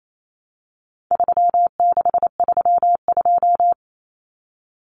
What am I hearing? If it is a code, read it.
Morse code, heard as a keyed tone, about 28 wpm, 700 Hz. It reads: 3632
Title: Common Cut Numbers - 4-Digit 28wpm